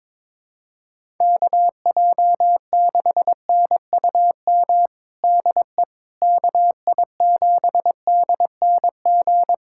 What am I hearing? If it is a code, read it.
Morse code, heard as a keyed tone, about 22 wpm, 700 Hz. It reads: KJ6NUM DE KI7DNG
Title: USA Callsign DE Callsign - 6 Characters - 22wpm